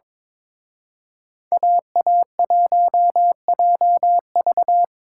Morse code, heard as a keyed tone, about 26 wpm, 700 Hz. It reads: AA1JV
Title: Speed Racing - USA 2x2 Call Signs 22wpm